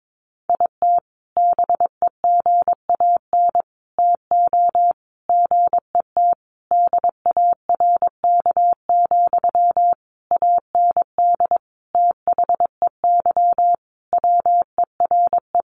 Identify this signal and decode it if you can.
Morse code, heard as a keyed tone, about 22 wpm, 700 Hz. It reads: IT BEGAN TO GET DARK, AND THEY WERE
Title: Sentences from Top 1400 Words 22wpm